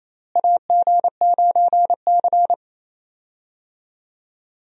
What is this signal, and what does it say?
Morse code, heard as a keyed tone, about 28 wpm, 700 Hz. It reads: AG9C
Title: USA Call Signs - 4 Characters - Repeated 2x - 28wpm